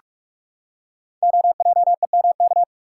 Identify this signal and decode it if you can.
Morse code, heard as a keyed tone, about 48 wpm, 700 Hz. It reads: OJEMK
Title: Code Groups - 5 Letters - 45wpm